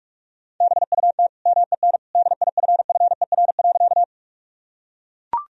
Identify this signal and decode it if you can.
Morse code, heard as a keyed tone, about 41 wpm, 700 Hz. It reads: BUT MEN DIFFER.
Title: Sentences from Top 1000 Words - Review Part 1 - 45wpm